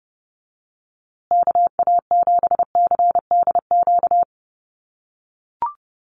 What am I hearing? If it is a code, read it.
Morse code, heard as a keyed tone, about 30 wpm, 700 Hz. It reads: KA7CDQ
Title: USA Call Signs Repeated 2x 30wpm